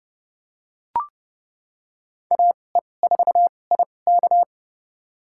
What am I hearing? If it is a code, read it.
Morse code, heard as a keyed tone, about 30 wpm, spaced as for 22 wpm, 700 Hz. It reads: AE4IK
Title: USA Call Signs - 5 Characters - 2x Character Spacing - 30wpm